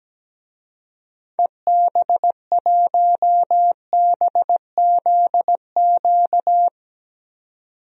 Morse code, heard as a keyed tone, about 17 wpm, 700 Hz. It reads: EB1BZQ
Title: DXCC-100 Call Signs - 6 Characters - 17wpm